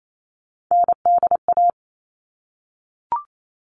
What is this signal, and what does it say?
Morse code, heard as a keyed tone, about 28 wpm, 700 Hz. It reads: NDA
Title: All Cut Numbers - 3-Digit 28wpm